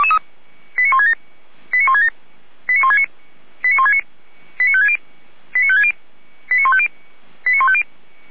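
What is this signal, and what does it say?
Five-tone selective call (ZVEI), heard as quick runs of five tones, tones of about 70 ms. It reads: 89167 89167 89169 89169 89560 89560 89150 89150
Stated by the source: ZVEI five-tone selective call 89220 received on an SDR.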